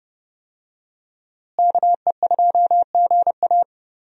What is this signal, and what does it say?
Morse code, heard as a keyed tone, about 31 wpm, 700 Hz. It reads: KE2GA
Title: USA Call Signs - 5 Characters - Repeated 3x - 30wpm